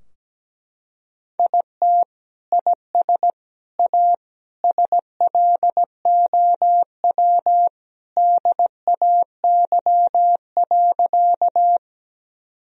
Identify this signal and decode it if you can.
Morse code, heard as a keyed tone, about 17 wpm, 700 Hz. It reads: IT IS A SLOW DAY.
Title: Sentences from Top 1000 Words - Review Part 2 - 17wpm